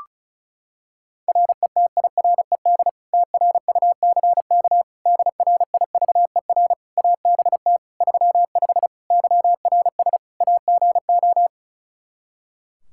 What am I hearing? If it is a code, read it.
Morse code, heard as a keyed tone, about 36 wpm, 700 Hz. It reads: RETIRED TRUCK DRIVER ABT 35 YRS AGO
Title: Practice QSO - Exchange 3 - More Info - 35wpm